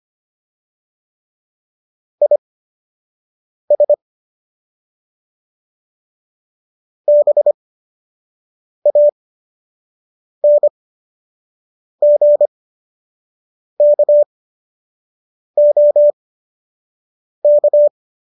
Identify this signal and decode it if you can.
Morse code, heard as a keyed tone, about 25 wpm, spaced as for 6 wpm, 590 Hz. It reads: IS BANGKOK